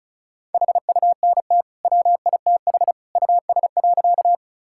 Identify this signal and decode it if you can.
Morse code, heard as a keyed tone, about 36 wpm, 700 Hz. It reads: HUNT WITH US.
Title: Sentences from Top 1600 Words 35wpm